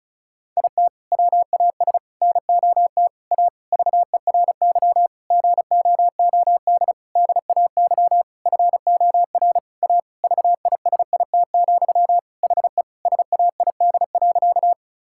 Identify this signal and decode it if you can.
Morse code, heard as a keyed tone, about 36 wpm, 700 Hz. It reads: IT WAS NOT A VERY GOOD DAY FOR A VISIT, HE SAID.
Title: Sentences from Top 1000 Words - Review Part 6 - 35wpm